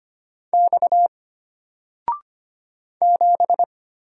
Morse code, heard as a keyed tone, about 25 wpm, 700 Hz. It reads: X 7
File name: Single-Letter-Number-25wpm